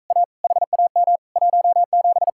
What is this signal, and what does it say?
Morse code, heard as a keyed tone, about 45 wpm, 700 Hz. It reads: A HAM 17